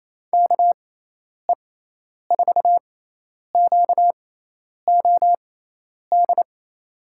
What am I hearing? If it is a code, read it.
Morse code, heard as a keyed tone, about 28 wpm, spaced as for 10 wpm, 700 Hz. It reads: KE4QOD
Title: USA Call Signs - 6 Characters - 6x Character Spacing - 28wpm